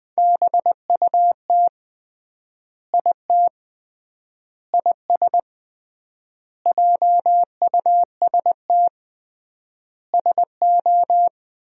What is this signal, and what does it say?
Morse code, heard as a keyed tone, about 21 wpm, 700 Hz. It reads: BUT IT IS JUST SO
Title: 3x Word-Spacing - Sentences from Top 200 Words - 20wpm